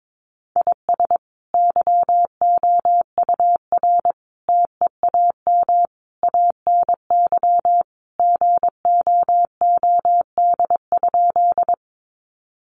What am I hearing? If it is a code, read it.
Morse code, heard as a keyed tone, about 22 wpm, 700 Hz. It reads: IS YOUR TEAM ANY GOOD?
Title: Sentences from Top 1500 Words 22wpm